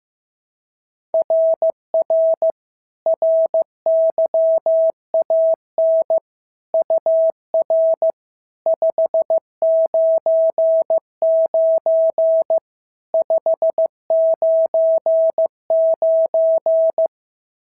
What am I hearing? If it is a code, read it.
Morse code, heard as a keyed tone, about 15 wpm, 650 Hz. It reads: RR RYAN UR 599 599
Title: SOTA QSO Practice 15wpm